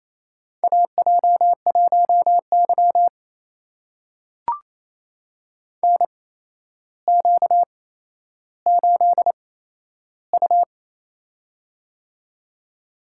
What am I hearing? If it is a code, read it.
Morse code, heard as a keyed tone, about 28 wpm, spaced as for 12 wpm, 700 Hz. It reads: AJ1Y NQ8U
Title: USA Call Signs - 4 Characters - 8x Character Spacing - 28wpm